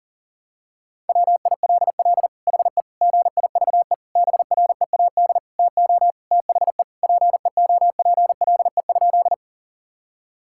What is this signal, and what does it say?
Morse code, heard as a keyed tone, about 40 wpm, 700 Hz. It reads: WILL HE GIVE BREAD TO THE PEOPLE?
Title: Sentences from Top 1600 Words 40wpm